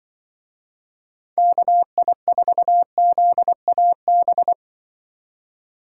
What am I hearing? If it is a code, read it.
Morse code, heard as a keyed tone, about 24 wpm, 700 Hz. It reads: KI4ZAB
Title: Speed Racing - USA 2x3 Call Signs 20wpm